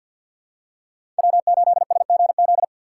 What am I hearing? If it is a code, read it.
Morse code, heard as a keyed tone, about 52 wpm, 690 Hz. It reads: W8SGZ
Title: USA Call Signs - 5 Characters - 4x Character Spacing - 50wpm